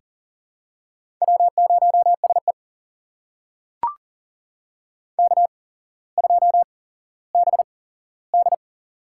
Morse code, heard as a keyed tone, about 38 wpm, spaced as for 17 wpm, 700 Hz. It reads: W0SE K2BD
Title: USA Call Signs - 4 Characters - 8x Character Spacing - 40wpm